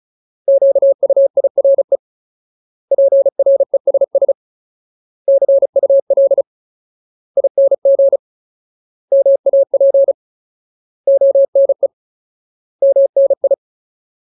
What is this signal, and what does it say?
Morse code, heard as a keyed tone, about 35 wpm, 550 Hz. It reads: QUIRE PRESS CUL ING MAP ONE MNI
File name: Ultimate Word List / 35WPM Ultimate